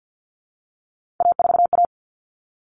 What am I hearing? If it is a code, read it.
Morse code, heard as a keyed tone, about 50 wpm, 700 Hz. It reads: A4A
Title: All Cut Numbers - 3-Digit 50wpm